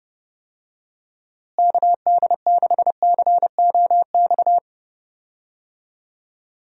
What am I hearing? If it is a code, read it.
Morse code, heard as a keyed tone, about 32 wpm, 700 Hz. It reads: KD6COX
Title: USA Call Signs - 6 Characters - Repeated 2x - 30wpm